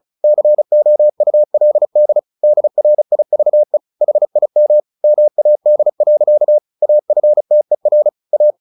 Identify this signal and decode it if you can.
Morse code, heard as a keyed tone, about 34 wpm, 600 Hz. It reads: COULD DRIVE HIM MAD. AFTER A
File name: the old man and the sea 2 35wpm